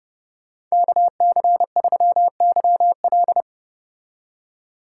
Morse code, heard as a keyed tone, about 35 wpm, 700 Hz. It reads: KC3YL